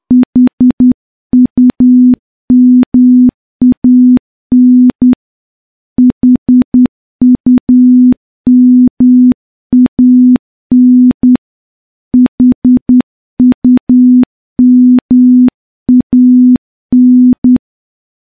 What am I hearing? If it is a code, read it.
Morse code, heard as a keyed tone, about 10 wpm, 250 Hz. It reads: HUMAN HUMAN HUMAN
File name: morse-training-10WPM